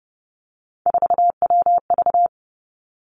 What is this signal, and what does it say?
Morse code, heard as a keyed tone, about 32 wpm, 700 Hz. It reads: 4WV
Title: Three Challenging Letter-Number Combinations 30wpm